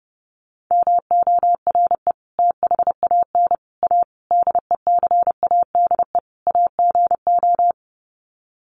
Morse code, heard as a keyed tone, about 30 wpm, 700 Hz. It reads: MORE THAN A DECADE AGO
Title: Sets of 5 Words Encore2 30wpm